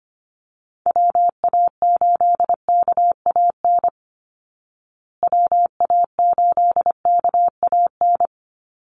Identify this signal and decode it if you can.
Morse code, heard as a keyed tone, about 25 wpm, 700 Hz. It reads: WA8KAN WA8KAN
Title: Call Signs Repeated 3x - 25wpm